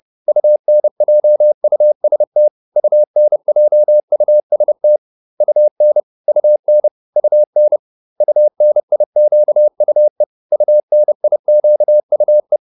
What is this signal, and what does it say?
Morse code, heard as a keyed tone, about 30 wpm, 600 Hz. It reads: UNJUST UNJUST UN UN UN UNIQUE UNIQUE
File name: UN6 30